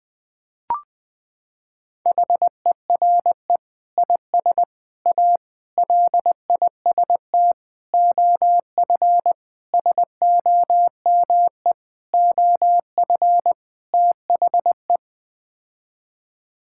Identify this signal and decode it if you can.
Morse code, heard as a keyed tone, about 21 wpm, 700 Hz. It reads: HERE IS A LIST OF SOME OF THE
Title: Sets of 8 Words - 20wpm